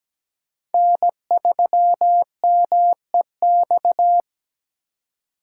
Morse code, heard as a keyed tone, about 17 wpm, 700 Hz. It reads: N3MEX
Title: USA Call Signs - 4x Character Spacing - 17wpm